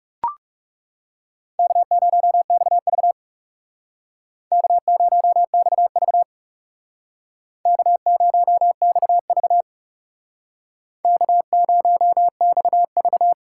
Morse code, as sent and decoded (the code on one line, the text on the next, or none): -.- ----- -..- ...- / -.- ----- -..- ...- / -.- ----- -..- ...- / -.- ----- -..- ...-
K0XV K0XV K0XV K0XV